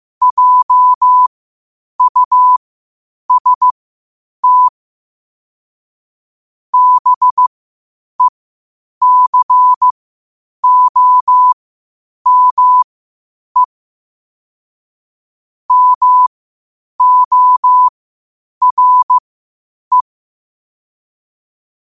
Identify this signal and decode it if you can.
Morse code, heard as a keyed tone, about 15 wpm, spaced as for 8 wpm, 1000 Hz. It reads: JUST BECOME MORE
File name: qotdfiles / 8wpm